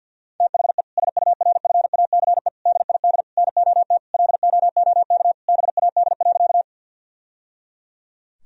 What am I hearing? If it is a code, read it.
Morse code, heard as a keyed tone, about 50 wpm, 700 Hz. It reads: THE SURFACE DID NOT LOOK BAD.
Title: Sentences from Top 1000 Words - Review Part 3 - 50wpm